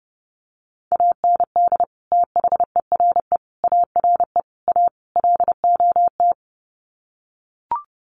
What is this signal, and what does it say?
Morse code, heard as a keyed tone, about 30 wpm, 700 Hz. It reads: AND THERE ARE A LOT
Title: Sets of 5 Words Encore2 30wpm